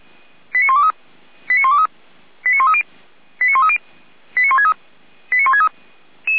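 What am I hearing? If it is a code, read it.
Five-tone selective call (ZVEI), heard as quick runs of five tones, tones of about 80 ms. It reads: 89123 89123 89130 89130 89163 89163
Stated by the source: ZVEI five-tone selective call 89119, received by SDR.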